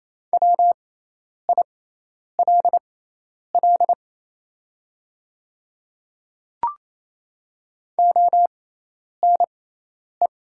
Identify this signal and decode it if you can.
Morse code, heard as a keyed tone, about 28 wpm, spaced as for 10 wpm, 700 Hz. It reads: WILL ONE
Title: 6x Character Spacing - Top 100 Words - 28wpm